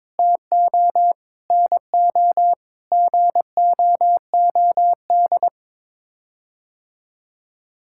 Morse code, heard as a keyed tone, about 21 wpm, 700 Hz. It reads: TO NO GOOD
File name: Sets of 9 Words - 22wpm